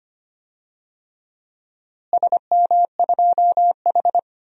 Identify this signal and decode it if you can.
Morse code, heard as a keyed tone, about 25 wpm, 700 Hz. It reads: SM2H
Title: DXCC-100 Call Signs - 4 Characters - Repeated 3x - 25wpm